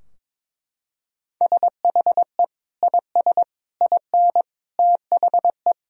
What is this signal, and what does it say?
Morse code, heard as a keyed tone, about 22 wpm, 700 Hz. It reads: SHE IS IN THE
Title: Sentences from Top 1000 Words - Review Part 4 - 22wpm